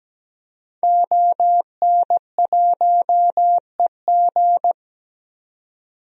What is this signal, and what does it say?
Morse code, heard as a keyed tone, about 17 wpm, 700 Hz. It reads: ON1EG
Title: DXCC-100 Call Signs - 5 Characters - 17wpm